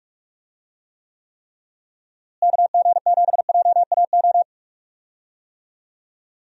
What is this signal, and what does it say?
Morse code, heard as a keyed tone, about 42 wpm, 700 Hz. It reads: KG7JAO
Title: USA Call Signs - 6 Characters - Repeated 3x - 45wpm